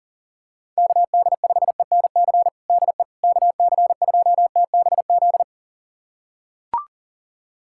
Morse code, heard as a keyed tone, about 40 wpm, 700 Hz. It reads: KD5ENC DE KC2TBZ
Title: USA Callsign DE Callsign - 6 Characters - 40wpm